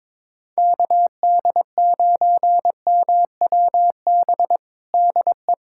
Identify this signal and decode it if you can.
Morse code, heard as a keyed tone, about 22 wpm, 700 Hz. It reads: KD9MWB DE
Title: USA Callsign DE Callsign - 6 Characters - 22wpm